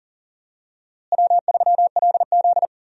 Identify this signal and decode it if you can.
Morse code, heard as a keyed tone, about 40 wpm, 700 Hz. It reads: W3LZ
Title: USA Call Signs - 4 Characters - Repeated 3x - 40wpm